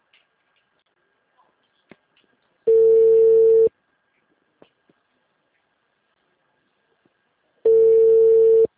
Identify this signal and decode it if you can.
Ringback tone (China, 450 Hz).
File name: out- -2005